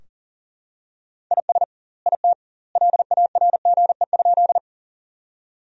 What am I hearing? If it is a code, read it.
Morse code, heard as a keyed tone, about 40 wpm, 700 Hz. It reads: IS IT LARGE?
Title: 2x Word-Spacing - Sentences from Top 200 Words - 40wpm